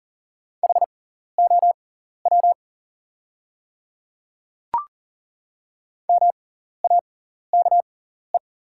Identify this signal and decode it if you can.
Morse code, heard as a keyed tone, about 38 wpm, spaced as for 14 wpm, 690 Hz. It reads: HOW MAKE
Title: 6x Character Spacing - Top 100 Words - 40wpm